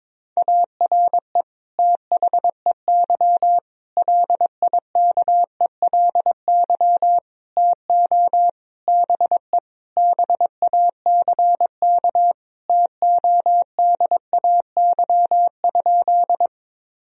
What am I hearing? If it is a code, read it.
Morse code, heard as a keyed tone, about 22 wpm, 700 Hz. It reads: ARE THEY LIKELY TO BE BACK TODAY?
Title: Sentences from Top 1000 Words - Review Part 4 - 22wpm